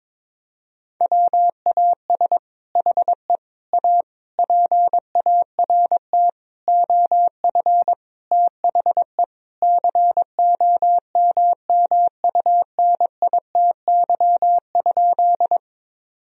Morse code, heard as a keyed tone, about 22 wpm, 700 Hz. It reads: WAS HE A PART OF THE COMMUNITY?
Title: Sentences from Top 1000 Words - Review Part 4 - 22wpm